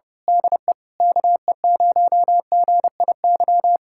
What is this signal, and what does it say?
Morse code, heard as a keyed tone, about 24 wpm, 700 Hz. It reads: DE KE0GIY
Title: Speed-Racing Common QSO Phrases - 25wpm